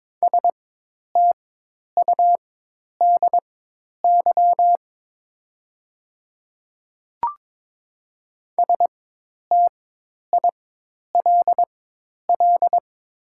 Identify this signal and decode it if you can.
Morse code, heard as a keyed tone, about 22 wpm, spaced as for 10 wpm, 690 Hz. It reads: STUDY STILL
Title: 4x Character Spacing - Top 300 Words - 22wpm